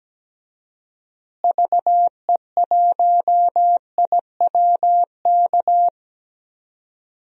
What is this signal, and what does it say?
Morse code, heard as a keyed tone, about 17 wpm, 700 Hz. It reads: VE1IWK
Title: DXCC-100 Call Signs - 6 Characters - Repeated 2x - 17wpm